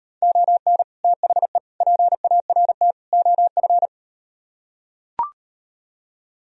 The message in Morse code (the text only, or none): ON THE PART OF